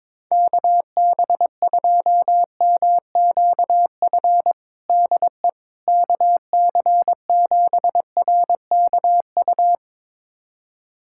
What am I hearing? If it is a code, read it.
Morse code, heard as a keyed tone, about 21 wpm, 700 Hz. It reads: KB2MQF DE KC7RKU
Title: USA Callsign DE Callsign 22wpm